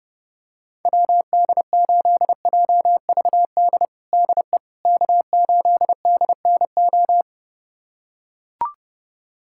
Morse code, heard as a keyed tone, about 30 wpm, 700 Hz. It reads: WD8JVD DE K8DNO
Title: USA Callsign DE Callsign 30wpm